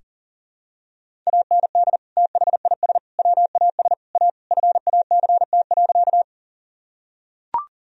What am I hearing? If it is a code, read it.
Morse code, heard as a keyed tone, about 40 wpm, 700 Hz. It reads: AND THIS WAS A FACT.
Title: Sentences from Top 1000 Words - Review Part 3 - 40wpm